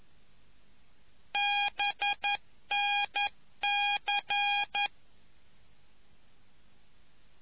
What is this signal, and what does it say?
Morse code, heard as a keyed tone, about 11 wpm, 820 Hz. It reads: BNC